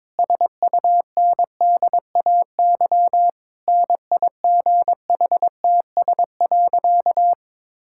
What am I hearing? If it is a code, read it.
Morse code, heard as a keyed tone, about 22 wpm, 700 Hz. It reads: SUNDAY NIGHTS.
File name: Difficult Sentences with Hints - 22wpm